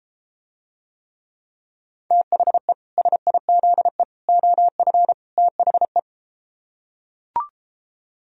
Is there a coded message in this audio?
Yes: Morse code from 2.1 to 6.0 s, heard as a keyed tone, about 33 wpm, 700 Hz. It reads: THE SIZE OF THE